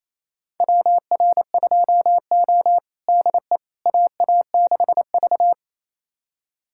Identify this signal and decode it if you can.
Morse code, heard as a keyed tone, about 28 wpm, 700 Hz. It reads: WR2O DE AA6V
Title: USA Callsign DE Callsign - 4 Characters - 28wpm